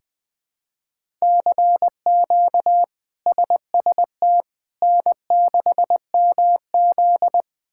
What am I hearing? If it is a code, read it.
Morse code, heard as a keyed tone, about 20 wpm, 700 Hz. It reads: CQ SST N6MZ
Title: K1USN SST QSO Practice - 20wpm